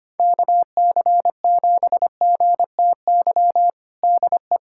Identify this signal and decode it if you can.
Morse code, heard as a keyed tone, about 25 wpm, 700 Hz. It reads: KC7GTY DE
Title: USA Callsign DE Callsign - 6 Characters - 25wpm